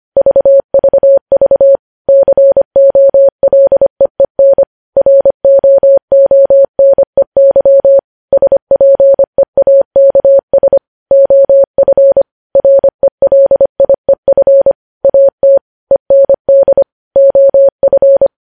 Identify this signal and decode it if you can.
Morse code, heard as a keyed tone, about 25 wpm, 550 Hz. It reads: VVV COLEEN ROONEY SPEAKS OF RELIEF AT END OF
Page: news headlines in morse code at 25 wpm